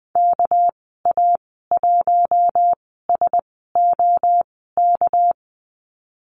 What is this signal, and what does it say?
Morse code, heard as a keyed tone, about 20 wpm, spaced as for 14 wpm, 700 Hz. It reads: KA1SOK
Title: 207 Lesson 33g - Letter j - Call Signs 20wpm